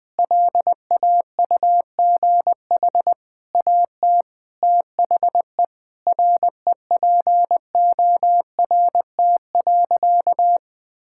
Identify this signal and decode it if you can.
Morse code, heard as a keyed tone, about 21 wpm, 700 Hz. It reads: LAUGH AT THE REPORT.
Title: Sentences from Top 1000 Words - Review Part 6 - 20wpm